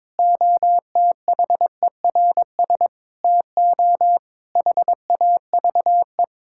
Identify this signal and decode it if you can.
Morse code, heard as a keyed tone, about 22 wpm, 700 Hz. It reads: OTHERS TO HAVE
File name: Difficult Sentences with Hints - 22wpm